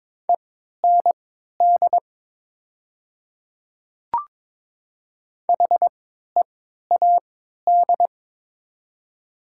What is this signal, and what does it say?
Morse code, heard as a keyed tone, about 22 wpm, spaced as for 12 wpm, 700 Hz. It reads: END HEAD
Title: 3x Character Spacing - Top 500 Words - 22wpm